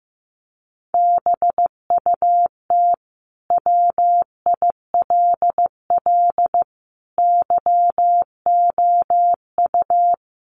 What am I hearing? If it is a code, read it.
Morse code, heard as a keyed tone, about 15 wpm, 700 Hz. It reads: BUT WILL YOU
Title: Sentences from Top 1400 Words 15wpm